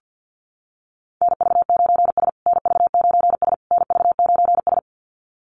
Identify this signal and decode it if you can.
Morse code, heard as a keyed tone, about 50 wpm, 700 Hz. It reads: NV9S NV9S NV9S
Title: USA Call Signs Repeated 3x 50wpm